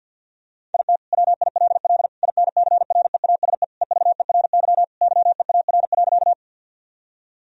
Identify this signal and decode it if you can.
Morse code, heard as a keyed tone, about 50 wpm, 700 Hz. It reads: IT WILL INCREASE EVERY YEAR.
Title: Sentences from Top 1000 Words - Review Part 4 - 50wpm